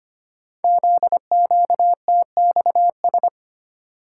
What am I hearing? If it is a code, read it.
Morse code, heard as a keyed tone, about 25 wpm, 700 Hz. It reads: ZQTXS
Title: Code Groups - 5 Letters - 25wpm